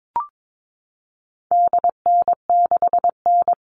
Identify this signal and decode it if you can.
Morse code, heard as a keyed tone, about 22 wpm, 700 Hz. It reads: DN6N
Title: All Cut Numbers - 4-Digit 22wpm